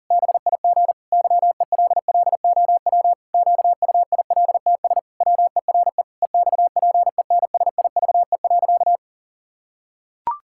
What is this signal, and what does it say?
Morse code, heard as a keyed tone, about 40 wpm, 700 Hz. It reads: BIG YELLOW QUILTS WERE EXPENSIVE.
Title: Pangram Challenge v2 - 40wpm